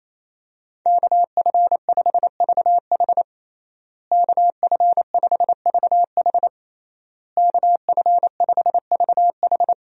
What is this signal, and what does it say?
Morse code, heard as a keyed tone, about 28 wpm, 700 Hz. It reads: KF5VH KF5VH KF5VH
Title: USA Call Signs - 5 Characters - Repeated 3x - 28wpm